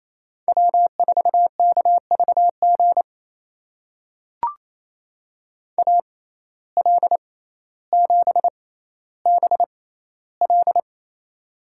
Morse code, heard as a keyed tone, about 28 wpm, spaced as for 14 wpm, 700 Hz. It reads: W4KVG AL7BL
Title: USA Call Signs - 5 Characters - 6x Character Spacing - 28wpm